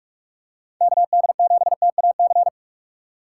Morse code, heard as a keyed tone, about 43 wpm, 700 Hz. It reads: KD7TAC
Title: USA Call Signs - 6 Characters - 4x Character Spacing - 45wpm